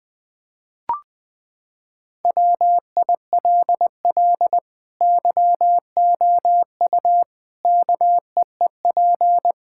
Morse code, heard as a keyed tone, about 20 wpm, 700 Hz. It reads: WILL YOU KEEP
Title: Sentences from Top 1600 Words 20wpm